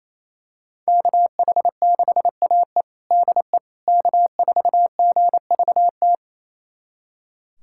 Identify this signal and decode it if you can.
Morse code, heard as a keyed tone, about 28 wpm, 700 Hz. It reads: KH6AE DE K4GVT
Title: USA Callsign DE Callsign - 5 Characters - 28wpm